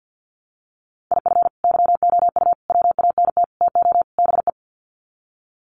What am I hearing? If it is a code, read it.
Morse code, heard as a keyed tone, about 50 wpm, 700 Hz. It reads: IF YOU WANT TO BE
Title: Sets of 5 Words Encore 50wpm